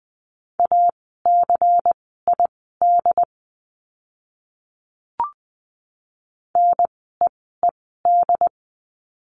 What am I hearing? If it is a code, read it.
Morse code, heard as a keyed tone, about 20 wpm, spaced as for 14 wpm, 700 Hz. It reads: ACID NEED
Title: 094 Lesson 16f - Letter c - Words 20wpm